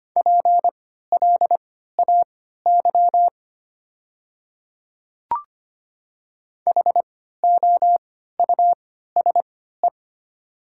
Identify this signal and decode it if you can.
Morse code, heard as a keyed tone, about 25 wpm, spaced as for 14 wpm, 700 Hz. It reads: PLAY HOUSE
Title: 3x Character Spacing - Top 200 Words - 25wpm